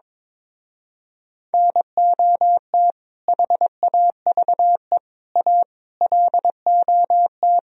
Morse code, heard as a keyed tone, about 22 wpm, 700 Hz. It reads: NOT HAVE A LOT
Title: Speed-Racing - Sets of 5 Words 22wpm